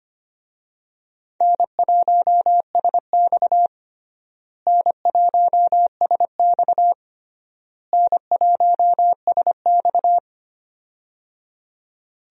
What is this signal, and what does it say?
Morse code, heard as a keyed tone, about 25 wpm, 700 Hz. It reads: N1SX N1SX N1SX
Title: USA Call Signs - 4 Characters - Repeated 3x - 25wpm